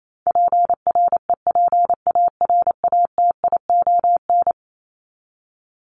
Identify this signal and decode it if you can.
Morse code, heard as a keyed tone, about 28 wpm, 700 Hz. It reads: PREPARATION
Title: Top 1251-1500 Words 28wpm